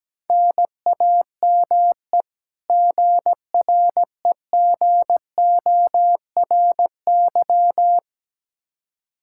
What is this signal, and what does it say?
Morse code, heard as a keyed tone, about 17 wpm, 700 Hz. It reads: NAME GREGORY
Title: Practice QSO - Exchange 1 - Beginning - 17wpm